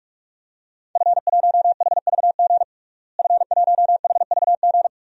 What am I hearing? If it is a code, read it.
Morse code, heard as a keyed tone, about 46 wpm, 700 Hz. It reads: F1HVG F1HVG
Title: DXCC-100 Call Signs - 5 Characters - Repeated 2x - 45wpm